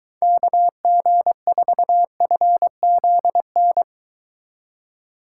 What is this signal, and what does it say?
Morse code, heard as a keyed tone, about 23 wpm, 700 Hz. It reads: KG4FZN